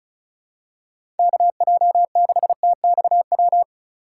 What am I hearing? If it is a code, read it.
Morse code, heard as a keyed tone, about 36 wpm, 700 Hz. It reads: KJ6TXW
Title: USA Call Signs - 2x Character Spacing - 35wpm